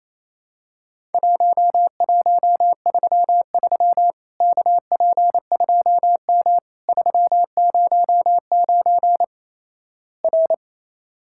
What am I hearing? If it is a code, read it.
Morse code, heard as a keyed tone, about 28 wpm, 700 Hz. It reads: 1133 KP2M 309 R
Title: Worked All Europe DX Contest - QTC Exchange - 28wpm